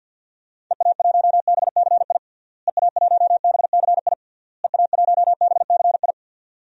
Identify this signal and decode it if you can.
Morse code, heard as a keyed tone, about 51 wpm, 700 Hz. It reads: EA1BCI EA1BCI EA1BCI
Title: DXCC-100 Call Signs - 6 Characters - Repeated 3x - 50wpm